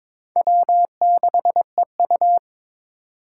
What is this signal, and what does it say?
Morse code, heard as a keyed tone, about 21 wpm, 700 Hz. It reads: W6EU
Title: K1USN SST QSO Practice - 22wpm